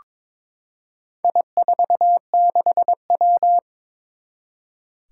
Morse code, heard as a keyed tone, about 22 wpm, 700 Hz. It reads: I46W
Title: Four Challenging Letter-Number Combinations 22wpm